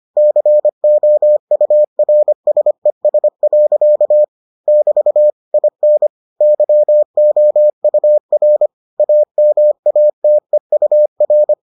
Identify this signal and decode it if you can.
Morse code, heard as a keyed tone, about 25 wpm, 600 Hz. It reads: COURSES. <BT> IN YOUR AMATEUR